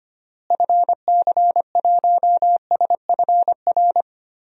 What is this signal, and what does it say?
Morse code, heard as a keyed tone, about 25 wpm, 700 Hz. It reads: FC1SFR
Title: DXCC-100 Call Signs - 6 Characters - Repeated 3x - 25wpm